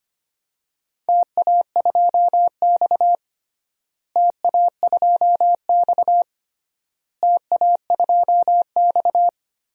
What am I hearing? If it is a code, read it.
Morse code, heard as a keyed tone, about 25 wpm, 700 Hz. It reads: TA2X TA2X TA2X
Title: DXCC-100 Call Signs - 4 Characters - Repeated 3x - 25wpm